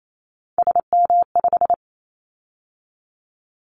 Morse code, heard as a keyed tone, about 28 wpm, 700 Hz. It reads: SM5
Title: Callsign Prefixes for Top 25 DXCC - 28wpm